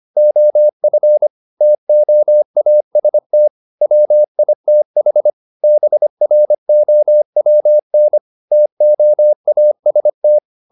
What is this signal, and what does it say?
Morse code, heard as a keyed tone, about 25 wpm, 600 Hz. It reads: OF TOAST WITH BROWN TOAST